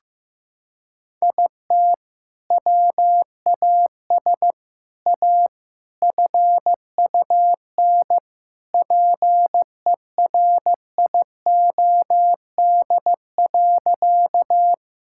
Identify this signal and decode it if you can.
Morse code, heard as a keyed tone, about 15 wpm, 700 Hz. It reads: IT WAS A FUN PERIOD.
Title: Sentences from Top 1000 Words - Review Part 5 - 15wpm